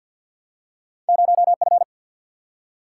Morse code, heard as a keyed tone, about 47 wpm, 700 Hz. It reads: 0F
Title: Single-Letter Single-Number 50wpm